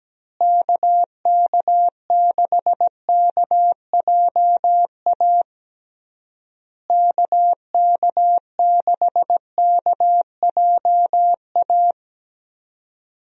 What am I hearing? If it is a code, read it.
Morse code, heard as a keyed tone, about 17 wpm, 700 Hz. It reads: KK6KJA KK6KJA
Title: USA Call Signs - 6 Characters - Repeated 3x - 17wpm